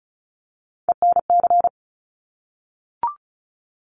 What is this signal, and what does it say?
Morse code, heard as a keyed tone, about 36 wpm, 700 Hz. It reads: ENC
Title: Top Three Letter Combinations in Words 35wpm